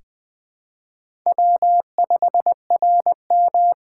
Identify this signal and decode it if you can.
Morse code, heard as a keyed tone, about 20 wpm, 700 Hz. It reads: W5RM
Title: USA Call Signs - 4 Characters - 6x Character Spacing - 20wpm